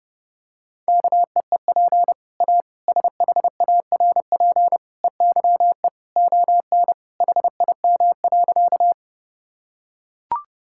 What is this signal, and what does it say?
Morse code, heard as a keyed tone, about 32 wpm, 700 Hz. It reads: KEEP A SHARP EYE ON HIM.
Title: Sentences from Top 1600 Words 30wpm